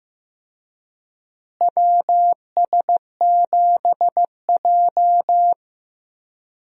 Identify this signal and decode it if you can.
Morse code, heard as a keyed tone, about 15 wpm, 700 Hz. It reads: WS7J
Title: USA Call Signs - 4 Characters - Repeated 3x - 15wpm